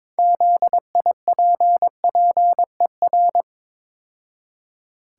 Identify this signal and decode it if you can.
Morse code, heard as a keyed tone, about 22 wpm, 700 Hz. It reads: ZIPPER
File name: Easy Words - Speed-Racing with Hints - 22wpm